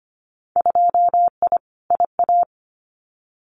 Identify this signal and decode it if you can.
Morse code, heard as a keyed tone, about 25 wpm, 700 Hz. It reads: 2I IA
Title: Winter Field Day Exchanges Encore 25wpm